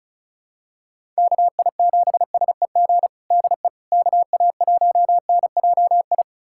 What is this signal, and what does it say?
Morse code, heard as a keyed tone, about 36 wpm, 700 Hz. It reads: KI7SEG DE KA1NJI
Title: USA Callsign DE Callsign - 6 Characters - 35wpm